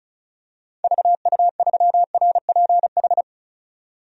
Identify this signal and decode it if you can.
Morse code, heard as a keyed tone, about 36 wpm, 700 Hz. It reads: VU3RPH
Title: DXCC-100 Call Signs - 6 Characters - 35wpm